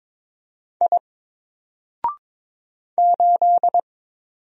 Morse code, heard as a keyed tone, about 22 wpm, 700 Hz. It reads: I 8
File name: Single-Letter-Number-22wpm